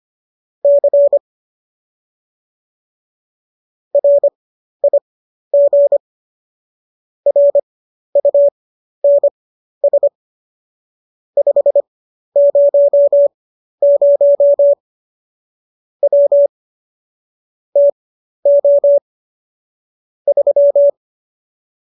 Morse code, heard as a keyed tone, about 25 wpm, spaced as for 12 wpm, 570 Hz. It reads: C RIG RUNS 500 W TO 3
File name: qso012 12